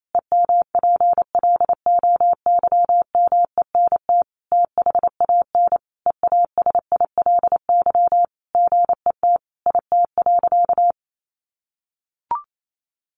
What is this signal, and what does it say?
Morse code, heard as a keyed tone, about 28 wpm, 700 Hz. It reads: EMPLOYMENT THAN EASILY GET IT.
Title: Sentences from Top 1500 Words 28wpm